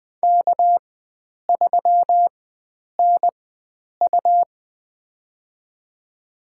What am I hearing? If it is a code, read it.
Morse code, heard as a keyed tone, about 20 wpm, spaced as for 9 wpm, 700 Hz. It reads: K3NU
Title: USA Call Signs - 4 Characters - 4x Character Spacing - 20wpm